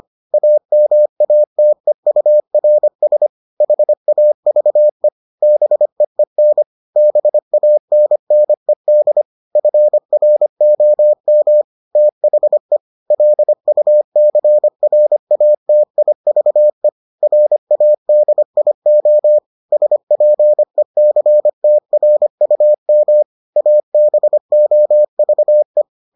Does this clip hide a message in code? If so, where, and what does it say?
Yes: Morse code from 0.3 to 25.8 s, heard as a keyed tone, about 25 wpm, 600 Hz. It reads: AMATEURS HAVE BEEN BANNED FROM THE LUCRATIVE RADIO SPECTRUM ABOVE